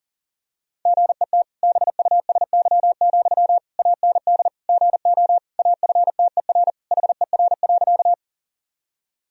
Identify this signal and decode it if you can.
Morse code, heard as a keyed tone, about 40 wpm, 700 Hz. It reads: GET BUSY, AND GO AFTER HER.
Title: Sentences from Top 1000 Words - Review Part 3 - 40wpm